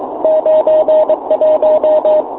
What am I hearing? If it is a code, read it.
Morse code, heard as a keyed tone, about 23 wpm, 600 Hz, with radio noise. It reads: 91
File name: fo29bake